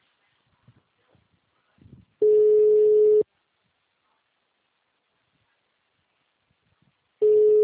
Ringback tone (Europe, 425 Hz).
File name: out- -2043- - - .43018